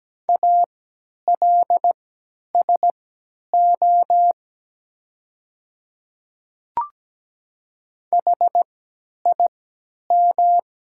Morse code, heard as a keyed tone, about 17 wpm, spaced as for 10 wpm, 700 Hz. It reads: ALSO HIM
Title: 3x Character Spacing - Top 200 Words - 17wpm